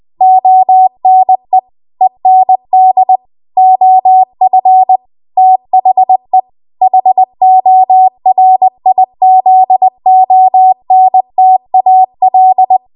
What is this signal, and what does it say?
Morse code, heard as a keyed tone, about 20 wpm, 750 Hz. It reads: ONE END OF THE HORIZONTAL
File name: Archive / 20 WPM / 20WPM